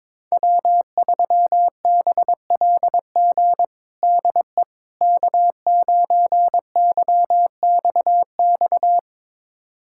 Morse code, heard as a keyed tone, about 21 wpm, 700 Hz. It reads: W3BLG DE K9YXX
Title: USA Callsign DE Callsign - 5 Characters - 22wpm